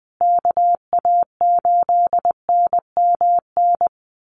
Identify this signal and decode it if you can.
Morse code, heard as a keyed tone, about 20 wpm, 700 Hz. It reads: KA8NMN
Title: USA Call Signs Repeated 3x 20wpm